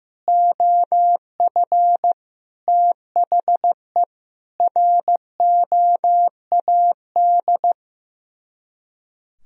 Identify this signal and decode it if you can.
Morse code, heard as a keyed tone, about 15 wpm, 700 Hz. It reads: OF THE ROAD
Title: Speed-Racing - Sets of 5 Words 15wpm